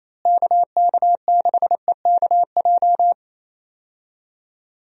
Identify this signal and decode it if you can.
Morse code, heard as a keyed tone, about 28 wpm, 700 Hz. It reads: KK6EKJ
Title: USA Call Signs - 6 Characters - Repeated 3x - 28wpm